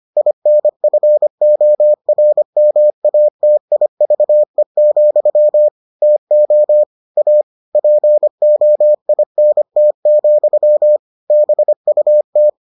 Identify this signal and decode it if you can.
Morse code, heard as a keyed tone, about 25 wpm, 600 Hz. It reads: INFORMATIVE, TO A POINT, BUT